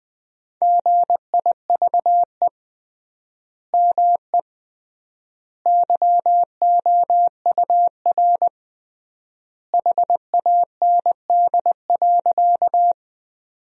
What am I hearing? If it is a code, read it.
Morse code, heard as a keyed tone, about 20 wpm, 700 Hz. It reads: GIVE ME YOUR HAND.
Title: 3x Word-Spacing - Sentences from Top 200 Words - 20wpm